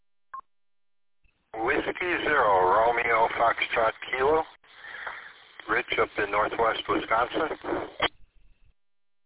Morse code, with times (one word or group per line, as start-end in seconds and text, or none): none